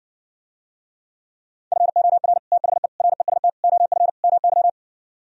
Morse code, heard as a keyed tone, about 60 wpm, 700 Hz. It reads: FOR THE REST OF MY